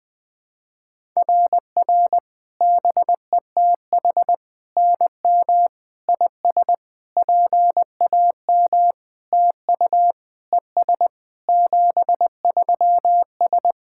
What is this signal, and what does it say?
Morse code, heard as a keyed tone, about 21 wpm, 700 Hz. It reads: RR BETH NM IS PAM TU ES 73S
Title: SOTA QSO Practice 20wpm